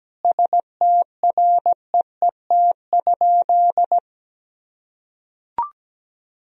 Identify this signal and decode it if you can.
Morse code, heard as a keyed tone, about 17 wpm, 700 Hz. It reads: STREET?
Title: Sentences from Top 1000 Words - Review Part 2 - 17wpm